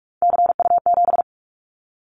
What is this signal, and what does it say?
Morse code, heard as a keyed tone, about 44 wpm, 700 Hz. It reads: CU7
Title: Callsign Prefixes for Top 100 DXCC - 45wpm